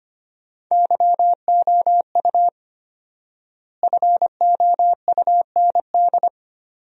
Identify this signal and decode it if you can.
Morse code, heard as a keyed tone, about 25 wpm, 700 Hz. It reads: YOU FOUND
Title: 4x Word-Spacing - Sentences from Top 300 Words - 25wpm